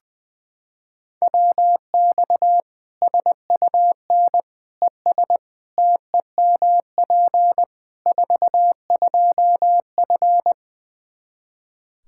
Morse code, heard as a keyed tone, about 20 wpm, 700 Hz. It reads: WX SUN ES TEMP 42F
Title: K1USN SST QSO Practice 20wpm